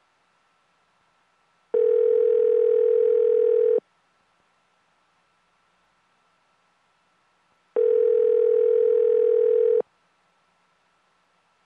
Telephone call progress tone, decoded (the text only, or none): ringback tone